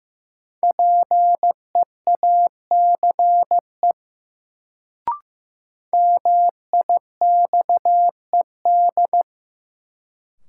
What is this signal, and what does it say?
Morse code, heard as a keyed tone, about 15 wpm, 700 Hz. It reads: PEACE MIXED
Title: Sentences from Top 1500 Words - New Word Drill Alt - 15wpm